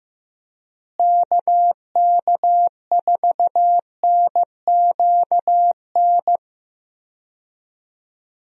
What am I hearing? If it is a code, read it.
Morse code, heard as a keyed tone, about 15 wpm, 700 Hz. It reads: KK4NQN
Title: USA Call Signs - 6 Characters - Repeated 3x - 15wpm